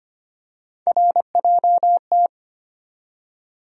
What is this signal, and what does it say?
Morse code, heard as a keyed tone, about 25 wpm, 700 Hz. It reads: RJT
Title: USA Call Sign Postfixes - 3 Letters 25wpm